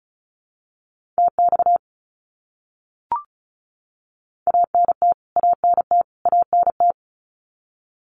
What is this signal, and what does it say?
Morse code, heard as a keyed tone, about 34 wpm, 700 Hz. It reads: TX ANT ANT ANT
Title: Abbreviations Repeated 3x - 35wpm